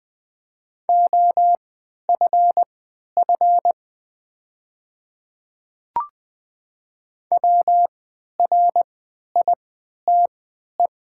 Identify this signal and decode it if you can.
Morse code, heard as a keyed tone, about 20 wpm, spaced as for 11 wpm, 700 Hz. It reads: OFF WRITE
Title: 3x Character Spacing - Top 200 Words - 20wpm